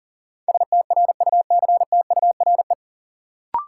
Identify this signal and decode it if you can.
Morse code, heard as a keyed tone, about 40 wpm, 700 Hz. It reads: STRUCTURE
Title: Sentences from Top 1000 Words - Review Part 6 - New Word Drill - 40wpm